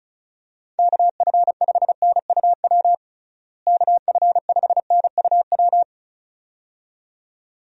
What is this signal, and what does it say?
Morse code, heard as a keyed tone, about 34 wpm, 700 Hz. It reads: KF5NUW KF5NUW
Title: USA Call Signs - 6 Characters - Repeated 2x - 35wpm